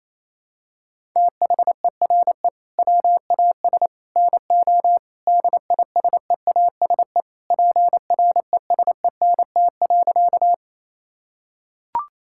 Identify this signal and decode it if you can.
Morse code, heard as a keyed tone, about 28 wpm, 700 Hz. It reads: THERE WAS NO DISEASE PRESENT.
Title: Sentences from Top 1000 Words - Review Part 6 - 28wpm